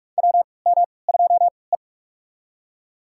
Morse code, heard as a keyed tone, about 43 wpm, spaced as for 25 wpm, 700 Hz. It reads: WM2E
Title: USA Call Signs - 4 Characters - 3x Character Spacing - 45wpm